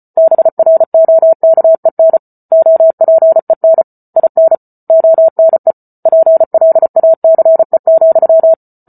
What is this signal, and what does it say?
Morse code, heard as a keyed tone, about 34 wpm, 650 Hz. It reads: BROKEN OPEN IN ONE PLACE,